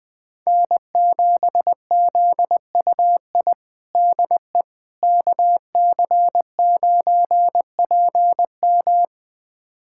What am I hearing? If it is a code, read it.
Morse code, heard as a keyed tone, about 20 wpm, 700 Hz. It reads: N7ZUI DE KC9PM
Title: USA Callsign DE Callsign - 5 Characters - 20wpm